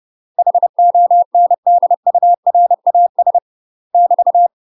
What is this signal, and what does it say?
Morse code, heard as a keyed tone, about 30 wpm, 700 Hz. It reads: HONDURAS <BT>
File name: UK MASTER QSOs TRAINING / 30WPM / UK MASTER QSOs 0033